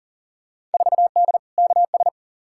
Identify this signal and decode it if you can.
Morse code, heard as a keyed tone, about 40 wpm, 700 Hz. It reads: 4D KS
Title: Field Day Exchanges Encore - 40wpm